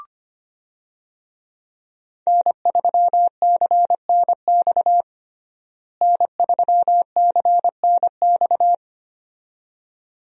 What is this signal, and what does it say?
Morse code, heard as a keyed tone, about 25 wpm, 700 Hz. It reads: N3CNX N3CNX
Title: USA Call Signs - 5 Characters - Repeated 2x - 25wpm